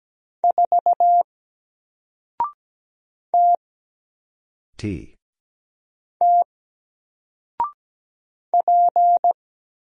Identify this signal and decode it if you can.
Morse code, heard as a keyed tone, about 17 wpm, 700 Hz. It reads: 4 T T P
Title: Single Letter-Number Alt 17wpm